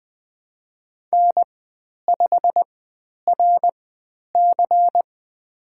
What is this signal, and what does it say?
Morse code, heard as a keyed tone, about 20 wpm, spaced as for 10 wpm, 700 Hz. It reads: N5RC
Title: 109 Lesson 18g - Letter m - Call Signs 20wpm